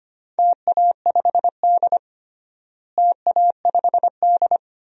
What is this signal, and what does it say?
Morse code, heard as a keyed tone, about 25 wpm, 700 Hz. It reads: TA5D TA5D
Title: DXCC-100 Call Signs - 4 Characters - Repeated 2x - 25wpm